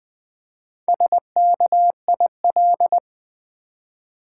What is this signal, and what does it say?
Morse code, heard as a keyed tone, about 20 wpm, 700 Hz. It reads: SKIL
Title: Build A Word - 1 Letter At A Time - 5-Letter Words - 20wpm